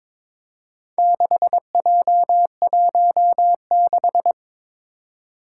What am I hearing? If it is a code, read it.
Morse code, heard as a keyed tone, about 22 wpm, 700 Hz. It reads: 6J16
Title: Four Challenging Letter-Number Combinations 22wpm